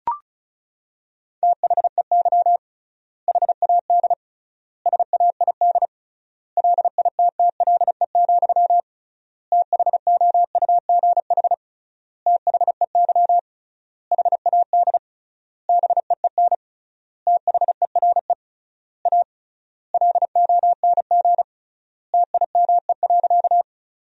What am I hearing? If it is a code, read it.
Morse code, heard as a keyed tone, about 34 wpm, 700 Hz. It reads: THEY HAD SAID LITTLE, THOUGH THEY HAD BEEN THERE A LONG TIME.
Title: 3x Word-Spacing - Sentences from Top 400 Words - 35wpm